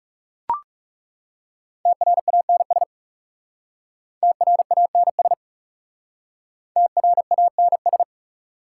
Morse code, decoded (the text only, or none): TRANS TRANS TRANS